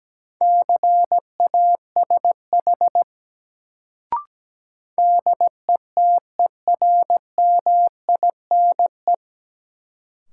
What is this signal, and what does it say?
Morse code, heard as a keyed tone, about 17 wpm, 700 Hz. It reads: CASH DETERMINE
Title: Sentences from Top 700 Words - New Word Drill - 17wpm